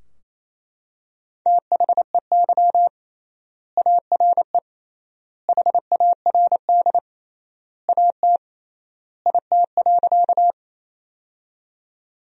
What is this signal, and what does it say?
Morse code, heard as a keyed tone, about 28 wpm, 700 Hz. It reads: THEY ARE HARD AT IT.
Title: 3x Word-Spacing - Sentences from Top 300 Words - 28wpm